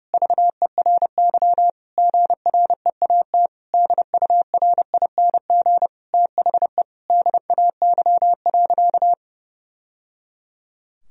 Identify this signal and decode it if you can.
Morse code, heard as a keyed tone, about 32 wpm, 700 Hz. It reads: VERY GREAT DURING THE DAY.
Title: Sentences from Top 1000 Words - Review Part 3 - 30wpm